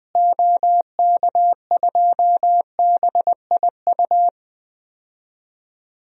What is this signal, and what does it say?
Morse code, heard as a keyed tone, about 20 wpm, 700 Hz. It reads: OK2BIU
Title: DXCC-100 Call Signs - 6 Characters - Repeated 2x - 20wpm